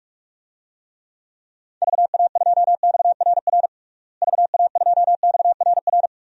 Morse code, heard as a keyed tone, about 46 wpm, 700 Hz. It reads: VA2XRR VA2XRR
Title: DXCC-100 Call Signs - 6 Characters - Repeated 3x - 45wpm